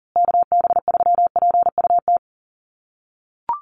Morse code, heard as a keyed tone, about 40 wpm, 700 Hz. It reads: KB3PUT
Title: USA Call Signs Repeated 3x 40wpm